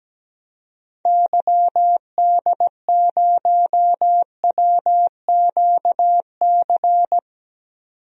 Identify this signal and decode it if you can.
Morse code, heard as a keyed tone, about 17 wpm, 700 Hz. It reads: YD0WQC
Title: DXCC-100 Call Signs - 6 Characters - 17wpm